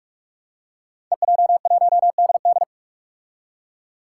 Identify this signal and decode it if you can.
Morse code, heard as a keyed tone, about 46 wpm, 700 Hz. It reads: EJ1DD
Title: Five Challenging Letter-Number Combinations 45wpm